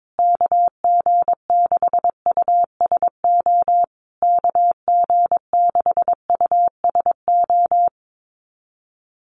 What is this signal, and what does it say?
Morse code, heard as a keyed tone, about 22 wpm, 700 Hz. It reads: KG6USO KG6USO
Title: USA Call Signs Repeated 3x 22wpm